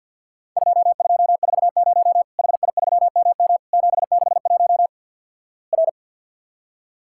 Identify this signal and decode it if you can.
Morse code, heard as a keyed tone, about 52 wpm, 700 Hz. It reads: 2240 HI3MM 761 R
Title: Worked All Europe DX Contest - QTC Exchange - 50wpm